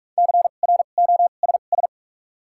Morse code, heard as a keyed tone, about 48 wpm, spaced as for 33 wpm, 700 Hz. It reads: CROSS
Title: 2x Character Spacing - Top 500 Words - 45wpm